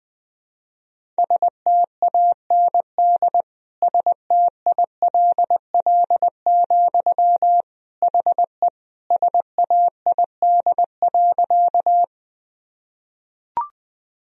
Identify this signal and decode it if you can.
Morse code, heard as a keyed tone, about 20 wpm, 700 Hz. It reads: STAND STILL, HE SAID.
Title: Sentences from Top 1000 Words - Review Part 1 - 20wpm